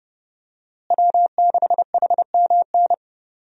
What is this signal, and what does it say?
Morse code, heard as a keyed tone, about 32 wpm, 700 Hz. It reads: W6HMN
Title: USA Call Signs - 5 Characters - Repeated 3x - 30wpm